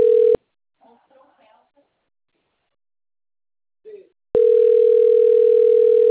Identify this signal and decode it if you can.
Ringback tone (North America), 440 + 480 Hz, with speech.
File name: rg-1600-unknown- - - .45481